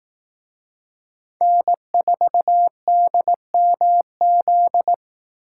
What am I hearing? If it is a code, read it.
Morse code, heard as a keyed tone, about 18 wpm, 700 Hz. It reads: N4DMZ
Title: Speed Racing - USA 1x3 Call Signs 15wpm